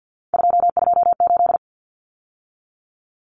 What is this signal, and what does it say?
Morse code, heard as a keyed tone, about 50 wpm, 700 Hz. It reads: 228
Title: Numbers - 3-Digit 50wpm